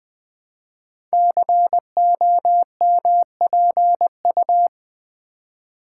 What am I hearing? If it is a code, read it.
Morse code, heard as a keyed tone, about 20 wpm, 700 Hz. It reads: COMPU
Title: Build A Word - 5 Letters At A Time - 10-Letter Words - 20wpm